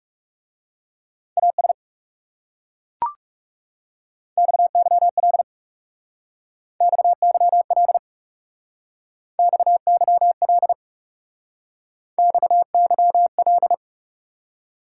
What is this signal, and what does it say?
Morse code, heard as a keyed tone, about 37 wpm, 700 Hz. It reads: AS XYL XYL XYL XYL
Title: Speed-Racing Top 180 QSO Elements 30wpm